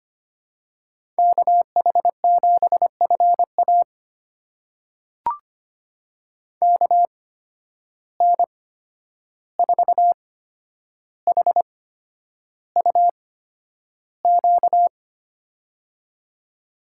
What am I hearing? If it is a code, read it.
Morse code, heard as a keyed tone, about 25 wpm, spaced as for 10 wpm, 700 Hz. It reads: KH7FA KN4HUQ
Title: USA Call Signs - 8x Character Spacing - 25wpm